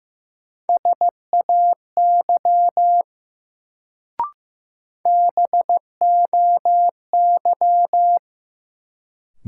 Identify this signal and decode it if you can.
Morse code, heard as a keyed tone, about 15 wpm, 700 Hz. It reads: SAY BOY
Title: Sentences from Top 200 Words - New Word Drill - 15wpm